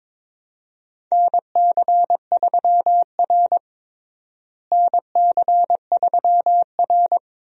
Morse code, heard as a keyed tone, about 22 wpm, 700 Hz. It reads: NC3R NC3R
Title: USA Call Signs - 4 Characters - Repeated 2x - 22wpm